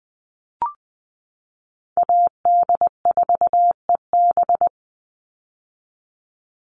Morse code, heard as a keyed tone, about 20 wpm, 700 Hz. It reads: AD4EB
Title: Call Signs - 20wpm